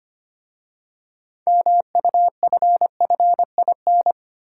Morse code, heard as a keyed tone, about 25 wpm, 700 Hz. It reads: MUFFIN
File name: practice / 25wpm / Easy Words with Hints - 25wpm